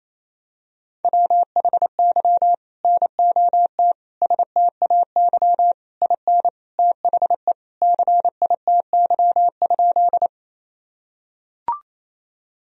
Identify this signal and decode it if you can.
Morse code, heard as a keyed tone, about 28 wpm, 700 Hz. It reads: WHY NOT STAY IN THE CITY?
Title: Sentences from Top 1000 Words - Review Part 1 - 28wpm